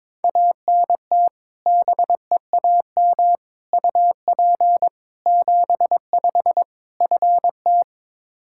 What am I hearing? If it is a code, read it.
Morse code, heard as a keyed tone, about 22 wpm, 700 Hz. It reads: ANT BEAM UP 75 FT
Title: Practice QSO - Exchange 2 - Info - 22wpm